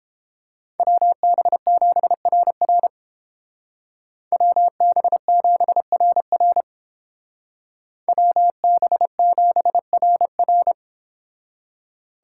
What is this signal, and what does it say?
Morse code, heard as a keyed tone, about 30 wpm, 700 Hz. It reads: WB7RR WB7RR WB7RR